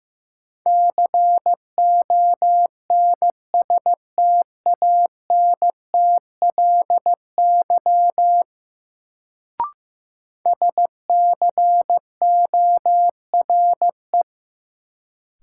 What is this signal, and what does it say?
Morse code, heard as a keyed tone, about 15 wpm, 700 Hz. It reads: CONSTANTLY SCORE
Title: Sentences from Top 1600 Words - New Word Drill Alt - 15wpm